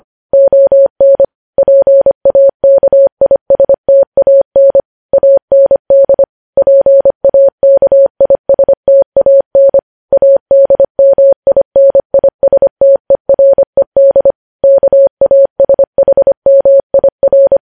Morse code, heard as a keyed tone, about 25 wpm, 550 Hz. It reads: ON PAKISTAN AND PAKISTAN ADMINISTERED KASHMIR
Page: news headlines in morse code at 25 wpm